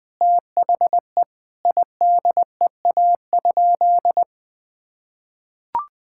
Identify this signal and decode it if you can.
Morse code, heard as a keyed tone, about 20 wpm, 700 Hz. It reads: THE IDEA?
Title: Sentences with 7-Word Phrases - 20wpm